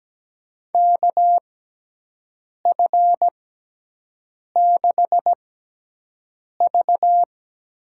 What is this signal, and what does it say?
Morse code, heard as a keyed tone, about 17 wpm, spaced as for 6 wpm, 700 Hz. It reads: KF6V
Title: USA Call Signs - 6x Character Spacing - 17wpm